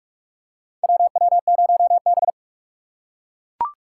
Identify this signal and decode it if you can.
Morse code, heard as a keyed tone, about 42 wpm, 700 Hz. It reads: WW0B
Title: USA Call Signs - 4 Characters - 2x Character Spacing - 45wpm